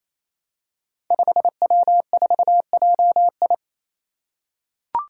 Morse code, heard as a keyed tone, about 27 wpm, 700 Hz. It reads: 5W4JI
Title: Five Challenging Letter-Number Combinations 28wpm